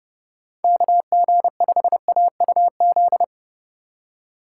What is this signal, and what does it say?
Morse code, heard as a keyed tone, about 31 wpm, 700 Hz. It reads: KG5AUZ